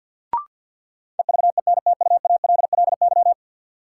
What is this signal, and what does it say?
Morse code, heard as a keyed tone, about 53 wpm, 700 Hz. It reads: EVENTUALLY
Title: Sentences from Top 1000 Words - Review Part 6 - New Word Drill - 50wpm